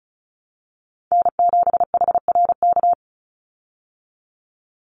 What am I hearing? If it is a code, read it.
Morse code, heard as a keyed tone, about 34 wpm, 700 Hz. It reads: N7HRK
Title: Call Signs Repeated 2x - 35wpm